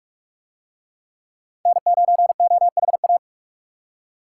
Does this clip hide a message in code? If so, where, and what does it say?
Yes: Morse code from 1.7 to 3.2 s, heard as a keyed tone, about 42 wpm, 700 Hz. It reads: N9OHA